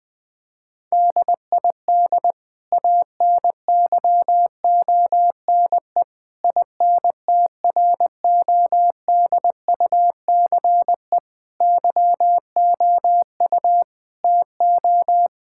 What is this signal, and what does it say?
Morse code, heard as a keyed tone, about 21 wpm, 700 Hz. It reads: DID ANYONE INTRODUCE YOU TO
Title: Sentences from Top 1600 Words 20wpm